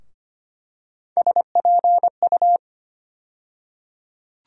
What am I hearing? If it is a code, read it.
Morse code, heard as a keyed tone, about 25 wpm, 700 Hz. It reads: SPU
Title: USA Call Sign Postfixes - 3 Letters 25wpm